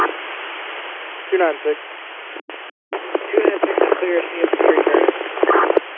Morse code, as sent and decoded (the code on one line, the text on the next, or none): none